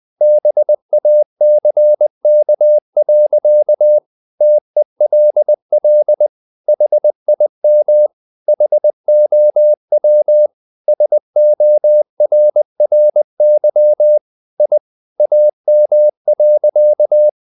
Morse code, heard as a keyed tone, about 20 wpm, 600 Hz. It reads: BACK. TELL HIM HOW SORRY I AM.
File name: the old man and the sea part 2 20wpm